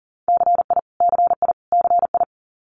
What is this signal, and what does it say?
Morse code, heard as a keyed tone, about 40 wpm, 700 Hz. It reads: CI CI CI
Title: Multi-letter Phonemes Repeated 3x 40wpm